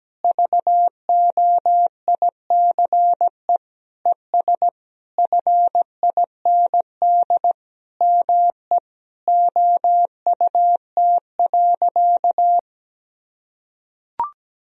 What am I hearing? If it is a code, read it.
Morse code, heard as a keyed tone, about 17 wpm, 700 Hz. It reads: VOICE ES FIND ME OUT.
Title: Sentences from Top 500 Words and Abbreviations 17wpm